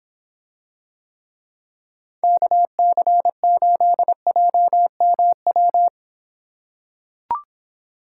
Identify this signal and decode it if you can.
Morse code, heard as a keyed tone, about 26 wpm, 700 Hz. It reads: KC8JMW